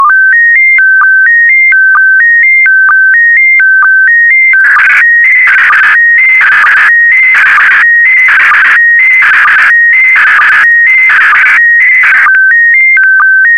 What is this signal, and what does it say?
SSTV (slow-scan television), PD 290.